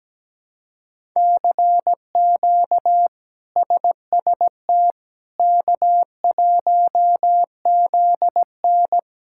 Morse code, heard as a keyed tone, about 17 wpm, 700 Hz. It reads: CQ SST K1ZN
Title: K1USN SST QSO Practice - 17wpm